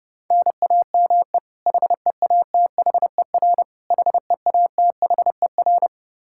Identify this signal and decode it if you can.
Morse code, heard as a keyed tone, about 28 wpm, 700 Hz. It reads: NAME HEATHER HEATHER
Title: K1USN SST QSO Practice 30wpm